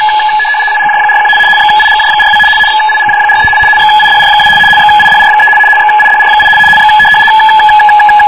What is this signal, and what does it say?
RTTY, about 75 baud. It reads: IGJ41 XIGJ42 XIGJ43 XIDR2 XIDR3 XIDR8 X